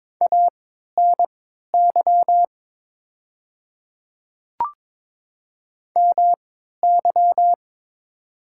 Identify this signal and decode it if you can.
Morse code, heard as a keyed tone, about 22 wpm, spaced as for 12 wpm, 700 Hz. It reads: ANY MY
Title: 3x Character Spacing - Top 200 Words - 22wpm